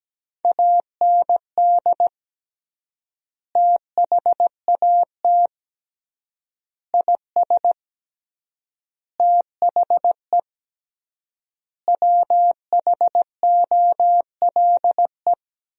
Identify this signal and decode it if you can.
Morse code, heard as a keyed tone, about 17 wpm, 700 Hz. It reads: AND THAT IS THE WHOLE
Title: 3x Word-Spacing - Sentences from Top 400 Words - 17wpm